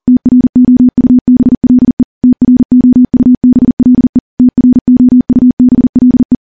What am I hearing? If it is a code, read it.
Morse code, heard as a keyed tone, about 43 wpm, 250 Hz. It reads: TROUBLE TROUBLE TROUBLE
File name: morse / morse-training-40WPM